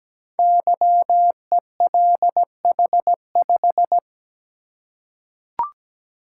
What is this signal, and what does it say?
Morse code, heard as a keyed tone, about 17 wpm, 700 Hz. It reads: YELH5
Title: Code Groups - 5 Letter:Numbers - 17wpm